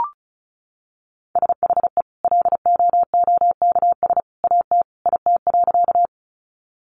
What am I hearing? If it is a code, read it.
Morse code, heard as a keyed tone, about 34 wpm, 700 Hz. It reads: SHE LOOKS AT IT.
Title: Additional Easy Phrases 35wpm